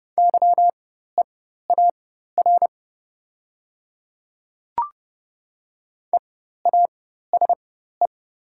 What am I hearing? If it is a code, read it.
Morse code, heard as a keyed tone, about 30 wpm, spaced as for 14 wpm, 690 Hz. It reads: YEAR EASE
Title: 4x Character Spacing - Top 300 Words - 30wpm